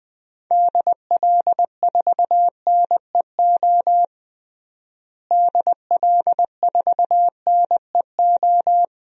Morse code, heard as a keyed tone, about 20 wpm, 700 Hz. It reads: DL4NEO DL4NEO
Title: DXCC-100 Call Signs - 6 Characters - Repeated 2x - 20wpm